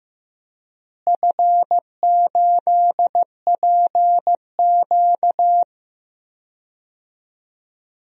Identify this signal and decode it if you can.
Morse code, heard as a keyed tone, about 15 wpm, 700 Hz. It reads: F8PQ
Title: DXCC-100 Call Signs - 4 Characters - Repeated 3x - 15wpm